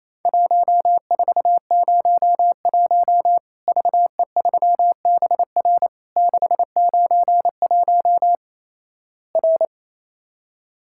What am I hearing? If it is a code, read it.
Morse code, heard as a keyed tone, about 28 wpm, 700 Hz. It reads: 1401 VE3BR 691 R
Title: Worked All Europe DX Contest - QTC Exchange - 28wpm